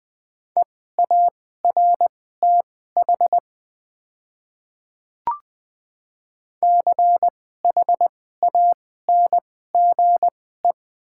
Morse code, heard as a keyed tone, about 20 wpm, spaced as for 14 wpm, 700 Hz. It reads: EARTH CHANGE
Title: 2x Character Spacing - Top 200 Words - 20wpm